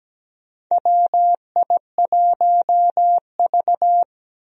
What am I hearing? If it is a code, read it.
Morse code, heard as a keyed tone, about 17 wpm, 700 Hz. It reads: WI1V
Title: USA Call Signs - 4 Characters - 17wpm